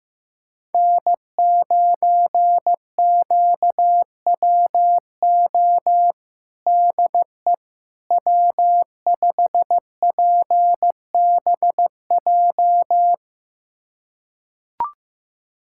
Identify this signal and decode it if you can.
Morse code, heard as a keyed tone, about 15 wpm, 700 Hz. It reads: N9QWO DE W5PBJ
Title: USA Callsign DE Callsign - 5 Characters - 15wpm